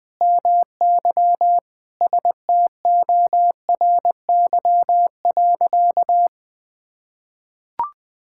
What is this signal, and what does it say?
Morse code, heard as a keyed tone, about 21 wpm, 700 Hz. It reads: MY STORY.
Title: Sentences from Top 1000 Words - Review Part 6 - 20wpm